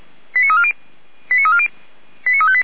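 Five-tone selective call (ZVEI), heard as quick runs of five tones, tones of about 70 ms. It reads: 89340 89340 89358
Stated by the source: ZVEI five-tone selective call received on SDR.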